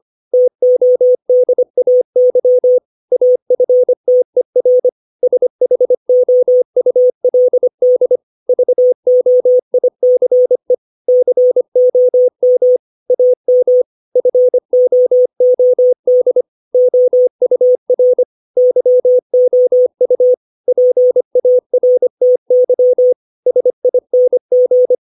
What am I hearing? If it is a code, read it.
Morse code, heard as a keyed tone, about 25 wpm, 500 Hz. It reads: TODAY AFTER SHOULD VOICE COM AM FOOD OUR YOU PARTY SING
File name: 632 words / 632 Words 25WPM